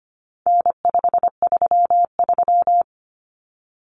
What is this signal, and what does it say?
Morse code, heard as a keyed tone, about 25 wpm, 700 Hz. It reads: N533
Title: Common Cut Numbers - 4-Digit 25wpm